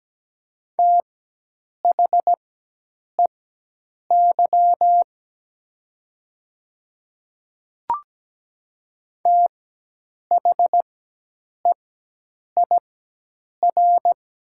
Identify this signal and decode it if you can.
Morse code, heard as a keyed tone, about 17 wpm, spaced as for 8 wpm, 700 Hz. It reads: THEY THEIR
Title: 4x Character Spacing - Top 100 Words - 17wpm